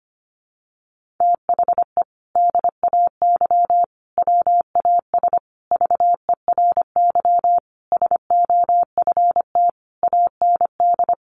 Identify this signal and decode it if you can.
Morse code, heard as a keyed tone, about 25 wpm, 700 Hz. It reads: THE DAY WAS VERY SOFT AND
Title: Sentences from Top 1500 Words 25wpm